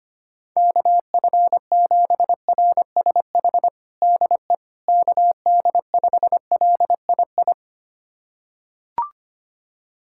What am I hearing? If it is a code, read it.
Morse code, heard as a keyed tone, about 25 wpm, 700 Hz. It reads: KF7RSH DE KD5LII
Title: USA Callsign DE Callsign 25wpm